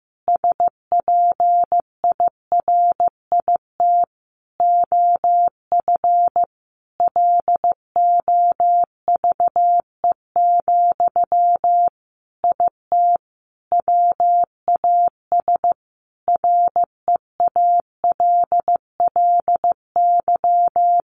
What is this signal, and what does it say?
Morse code, heard as a keyed tone, about 15 wpm, 700 Hz. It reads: SPIRIT OF LOVE, IT WAS REALLY
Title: Sentences from Top 1400 Words 15wpm